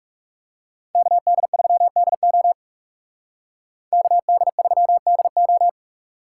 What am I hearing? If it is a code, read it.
Morse code, heard as a keyed tone, about 36 wpm, 700 Hz. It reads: KD3DO KD3DO
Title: Speed Racing - USA Call Signs 30wpm